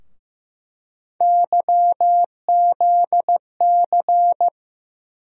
Morse code, heard as a keyed tone, about 15 wpm, 700 Hz. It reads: YZC
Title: USA Call Sign Postfixes - 3 Letters 15wpm